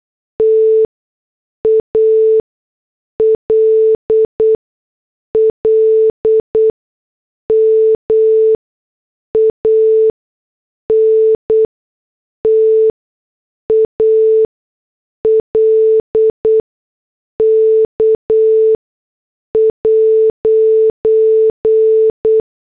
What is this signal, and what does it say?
Morse code, heard as a keyed tone, about 8 wpm, spaced as for 6 wpm, 440 Hz. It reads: TALLMAN TALK'